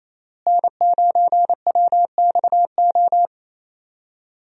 Morse code, heard as a keyed tone, about 28 wpm, 700 Hz. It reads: N9WXO
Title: USA Call Signs - 5 Characters - Repeated 3x - 28wpm